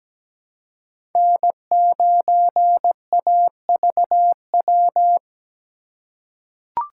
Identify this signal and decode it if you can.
Morse code, heard as a keyed tone, about 17 wpm, 700 Hz. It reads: N9AVW
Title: USA Call Signs - 5 Characters - 4x Character Spacing - 17wpm